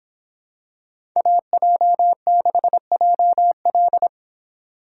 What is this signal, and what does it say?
Morse code, heard as a keyed tone, about 26 wpm, 700 Hz. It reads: AJ6JL